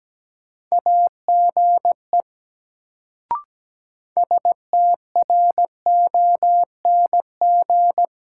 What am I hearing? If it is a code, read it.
Morse code, heard as a keyed tone, about 17 wpm, 700 Hz. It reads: AGE STRONG
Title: Sentences from Top 500 Words - New Word Drill - 17wpm